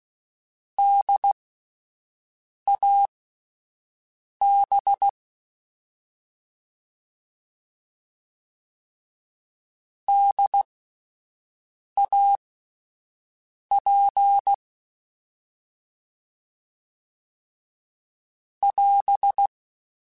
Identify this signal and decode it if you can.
Morse code, heard as a keyed tone, about 16 wpm, spaced as for 5 wpm, 800 Hz. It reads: DAB DAP <AS>